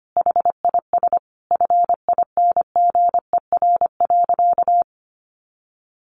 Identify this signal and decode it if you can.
Morse code, heard as a keyed tone, about 25 wpm, 700 Hz. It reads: HIS FINGER.
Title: Additional Easy Phrases 25wpm